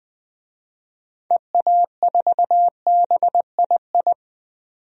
Morse code, heard as a keyed tone, about 20 wpm, 700 Hz. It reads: EA4BII
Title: DXCC-100 Call Signs - 6 Characters - 20wpm